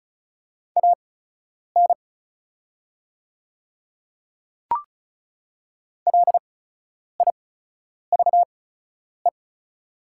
Morse code, heard as a keyed tone, about 35 wpm, spaced as for 10 wpm, 690 Hz. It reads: AN LIVE
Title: 8x Character Spacing - Top 300 Words - 35wpm